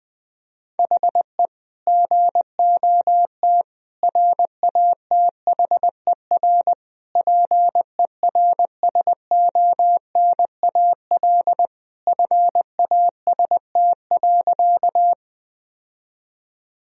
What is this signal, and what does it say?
Morse code, heard as a keyed tone, about 20 wpm, 700 Hz. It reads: HE GOT RATHER PERSONAL FAST.
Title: Sentences from Top 1000 Words - Review Part 4 - 20wpm